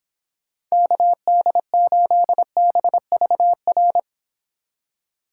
Morse code, heard as a keyed tone, about 26 wpm, 700 Hz. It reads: KD8BVR